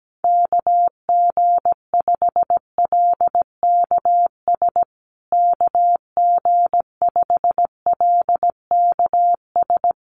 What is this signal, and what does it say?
Morse code, heard as a keyed tone, about 17 wpm, 700 Hz. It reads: KG5LKS KG5LKS
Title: USA Call Signs Repeated 3x 17wpm